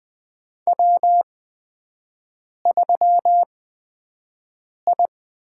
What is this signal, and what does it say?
Morse code, heard as a keyed tone, about 20 wpm, spaced as for 5 wpm, 700 Hz. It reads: W3I
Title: USA Call Signs - 5 Characters - 8x Character Spacing - 20wpm